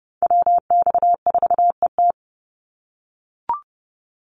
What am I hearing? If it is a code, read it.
Morse code, heard as a keyed tone, about 30 wpm, 700 Hz. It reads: WX4ET
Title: NA SOTA Chaser Callsigns 30wpm